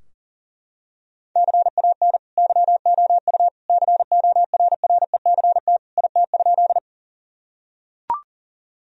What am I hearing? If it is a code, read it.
Morse code, heard as a keyed tone, about 40 wpm, 700 Hz. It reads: CAN YOU CORRECT IT?
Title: Sentences from Top 1000 Words - Review Part 2 - 40wpm